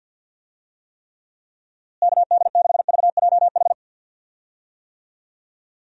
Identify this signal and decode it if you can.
Morse code, heard as a keyed tone, about 49 wpm, 700 Hz. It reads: KD6VJH
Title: USA Call Signs - 6 Characters - Repeated 2x - 50wpm